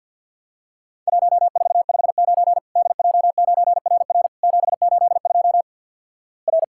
Morse code, heard as a keyed tone, about 50 wpm, 700 Hz. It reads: 1459 DJ9RR 782 R
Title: Worked All Europe DX Contest - QTC Exchange - 50wpm